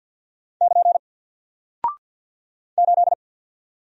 Morse code, heard as a keyed tone, about 48 wpm, 700 Hz. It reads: <KN> 8
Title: Single Character (Letter:Number:Punctuation:Prosign) 50wpm